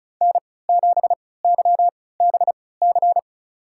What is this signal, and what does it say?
Morse code, heard as a keyed tone, about 36 wpm, spaced as for 20 wpm, 700 Hz. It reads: N7YBC
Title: USA Call Signs - 5 Characters - 3x Character Spacing - 35wpm